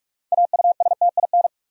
Morse code, heard as a keyed tone, about 46 wpm, 700 Hz. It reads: AUSTIN
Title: USA Common Names - 45wpm